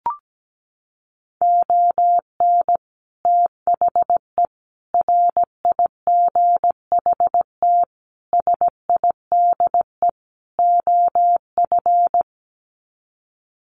Morse code, heard as a keyed tone, about 17 wpm, 700 Hz. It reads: ON THE RIGHT SIDE OF
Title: Sets of 5 Words Encore 17wpm